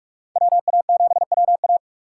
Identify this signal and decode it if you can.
Morse code, heard as a keyed tone, about 46 wpm, 700 Hz. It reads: WA7WA
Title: USA Call Signs - 5 Characters - 3x Character Spacing - 45wpm